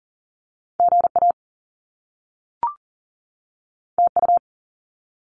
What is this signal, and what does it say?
Morse code, heard as a keyed tone, about 40 wpm, 700 Hz. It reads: GA TU
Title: Top Two Letter Combinations in Words 40wpm